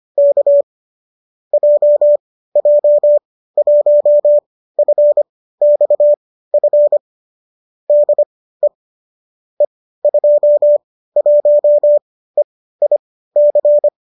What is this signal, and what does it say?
Morse code, heard as a keyed tone, about 25 wpm, spaced as for 15 wpm, 570 Hz. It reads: K JJ1FXF DE E21EIC